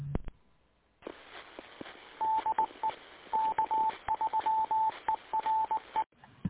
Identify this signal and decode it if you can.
Morse code, heard as a keyed tone, about 19 wpm, 880 Hz. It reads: DE K3ERE
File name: SwELNets / QSO 2022 08 19